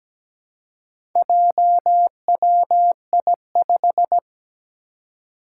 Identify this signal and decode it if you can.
Morse code, heard as a keyed tone, about 17 wpm, 700 Hz. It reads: JWI5
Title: Four Challenging Letter-Number Combinations 17wpm